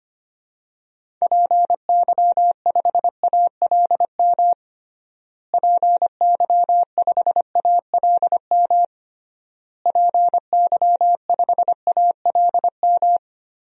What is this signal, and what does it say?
Morse code, heard as a keyed tone, about 25 wpm, 700 Hz. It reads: PY5ALM PY5ALM PY5ALM
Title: DXCC-100 Call Signs - 6 Characters - Repeated 3x - 25wpm